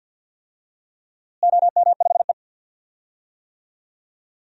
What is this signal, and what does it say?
Morse code, heard as a keyed tone, about 47 wpm, 700 Hz. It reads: OM5E
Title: DXCC-100 Call Signs - 4 Characters - Repeated 3x - 50wpm